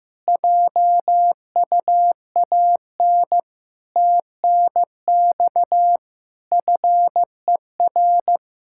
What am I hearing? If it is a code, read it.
Morse code, heard as a keyed tone, about 15 wpm, 700 Hz. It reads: JUAN TNX FER
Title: Practice QSO - Exchange 4 - Ending - 15wpm